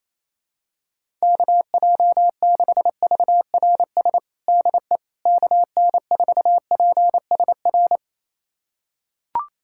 Morse code, heard as a keyed tone, about 27 wpm, 700 Hz. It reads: KJ6VRS DE KN4PSR
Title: USA Callsign DE Callsign - 6 Characters - 28wpm